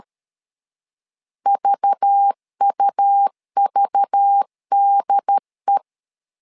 Morse code, heard as a keyed tone, about 13 wpm, 800 Hz. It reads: VUVDE
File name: LEZIONI / 5 LBVC / 5 LBV 12wpm 1855